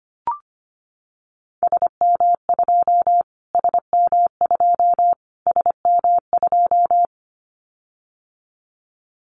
Morse code, heard as a keyed tone, about 25 wpm, 700 Hz. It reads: SM2 SM2 SM2
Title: Callsign Prefixes for Top 25 DXCC - Repeated 3x - 25wpm